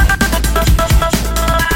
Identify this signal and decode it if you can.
There is no code.